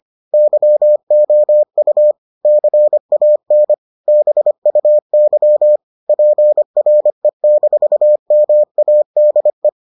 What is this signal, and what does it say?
Morse code, heard as a keyed tone, about 25 wpm, 600 Hz. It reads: YOU CAN BUY PRE-MADE